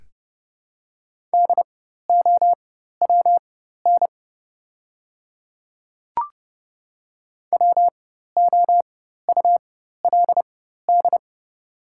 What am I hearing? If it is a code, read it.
Morse code, heard as a keyed tone, about 30 wpm, spaced as for 14 wpm, 700 Hz. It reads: DOWN WOULD
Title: 4x Character Spacing - Top 200 Words - 30wpm